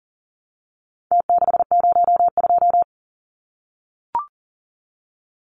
Morse code, heard as a keyed tone, about 40 wpm, 700 Hz. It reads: T602
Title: Numbers - Contest Style - 4-Digit 40wpm